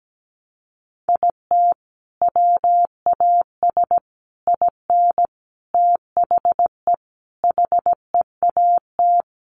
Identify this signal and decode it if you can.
Morse code, heard as a keyed tone, about 17 wpm, 700 Hz. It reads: IT WAS IN THE HEAT
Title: Sentences from Top 1300 Words 17wpm